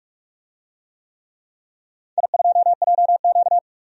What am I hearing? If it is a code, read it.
Morse code, heard as a keyed tone, about 42 wpm, 700 Hz. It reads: I2JQ
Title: DXCC-100 Call Signs - 4 Characters - 45wpm